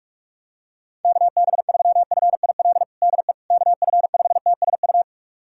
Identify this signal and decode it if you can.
Morse code, heard as a keyed tone, about 46 wpm, 700 Hz. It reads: KB3FIL DE KF5TSU
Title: USA Callsign DE Callsign - 6 Characters - 45wpm